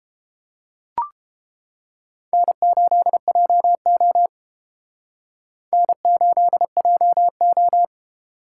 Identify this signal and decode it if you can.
Morse code, heard as a keyed tone, about 28 wpm, 700 Hz. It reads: N8JO N8JO